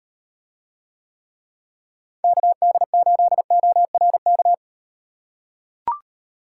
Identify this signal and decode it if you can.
Morse code, heard as a keyed tone, about 36 wpm, 700 Hz. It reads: KD8ORK